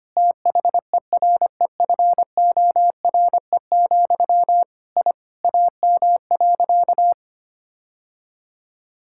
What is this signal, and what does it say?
Morse code, heard as a keyed tone, about 25 wpm, 700 Hz. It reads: THEREFORE, I AM.
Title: Sentences from Top 1000 Words - Review Part 5 - 25wpm